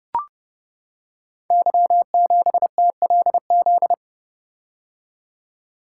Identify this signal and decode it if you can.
Morse code, heard as a keyed tone, about 30 wpm, 700 Hz. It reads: Y7TLZ
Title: Code Groups - 5 Letter:Numbers - 30wpm